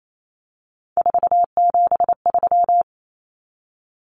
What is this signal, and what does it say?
Morse code, heard as a keyed tone, about 28 wpm, 700 Hz. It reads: 473
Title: Numbers - 3-Digit 28wpm